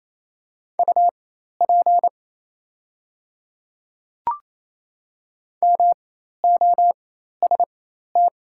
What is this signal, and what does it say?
Morse code, heard as a keyed tone, about 27 wpm, spaced as for 13 wpm, 690 Hz. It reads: UP MOST
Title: 4x Character Spacing - Top 100 Words - 28wpm